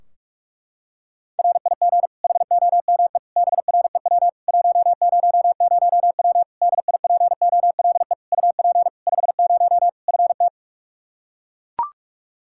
Morse code, heard as a keyed tone, about 41 wpm, 700 Hz. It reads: RIG HOME BREW 100W DIPOLE UP 50 FT
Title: Practice QSO - Exchange 2 - Info - 45wpm